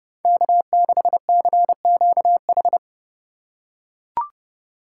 Morse code, heard as a keyed tone, about 29 wpm, 700 Hz. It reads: K6CQH